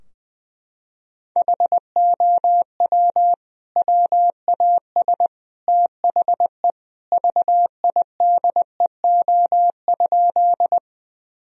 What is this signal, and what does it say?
Morse code, heard as a keyed tone, about 20 wpm, 700 Hz. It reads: HOW WAS THE VIDEO?
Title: Sentences from Top 1000 Words - Review Part 4 - 20wpm